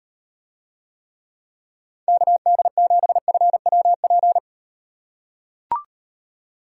Morse code, heard as a keyed tone, about 38 wpm, 700 Hz. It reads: KD7FWP